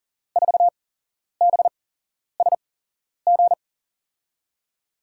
Morse code, heard as a keyed tone, about 38 wpm, spaced as for 11 wpm, 700 Hz. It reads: 4BSG
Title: USA Call Signs - 6 Characters - 8x Character Spacing - 40wpm